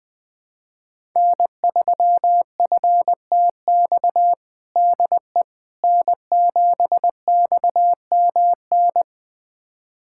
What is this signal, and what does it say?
Morse code, heard as a keyed tone, about 21 wpm, 700 Hz. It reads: N3FTX DE N7XMN
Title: USA Callsign DE Callsign - 5 Characters - 20wpm